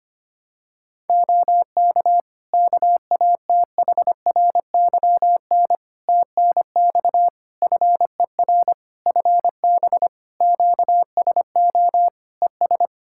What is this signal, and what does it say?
Morse code, heard as a keyed tone, about 25 wpm, 700 Hz. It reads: OK KATHRYN TNX FER FB QSO ES
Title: K1USN SST QSO Practice 25wpm